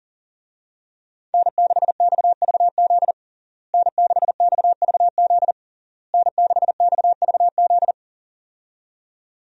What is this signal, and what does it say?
Morse code, heard as a keyed tone, about 40 wpm, 700 Hz. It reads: N6XVZ N6XVZ N6XVZ
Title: USA Call Signs - 5 Characters - Repeated 3x - 40wpm